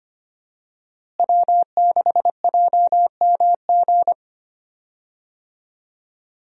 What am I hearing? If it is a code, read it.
Morse code, heard as a keyed tone, about 25 wpm, 700 Hz. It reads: W6JMG
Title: USA Call Signs - 5 Characters - Repeated 2x - 25wpm